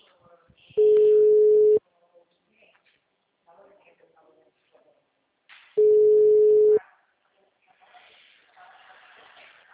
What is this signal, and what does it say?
Ringback tone (Europe, 425 Hz), with speech.
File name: out- -2016- - - .12892